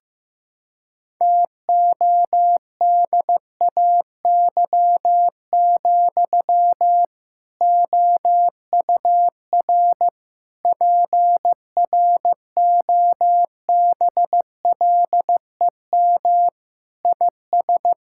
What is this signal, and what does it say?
Morse code, heard as a keyed tone, about 15 wpm, 700 Hz. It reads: TODAY, OUR PROBLEM IS
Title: Sentences from Top 1600 Words 15wpm